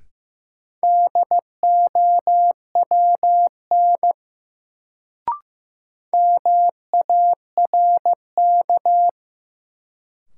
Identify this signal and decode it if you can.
Morse code, heard as a keyed tone, about 16 wpm, 700 Hz. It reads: DOWN MARK
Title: Sentences from Top 300 Words - New Word Drill - 15wpm